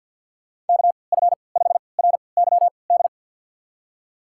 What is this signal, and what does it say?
Morse code, heard as a keyed tone, about 53 wpm, spaced as for 29 wpm, 700 Hz. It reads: KF5RYD
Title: USA Call Signs - 6 Characters - 3x Character Spacing - 50wpm